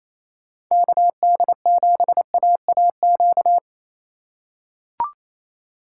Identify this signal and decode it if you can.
Morse code, heard as a keyed tone, about 28 wpm, 700 Hz. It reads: KD7AAQ
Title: USA Call Signs - 2x Character Spacing - 28wpm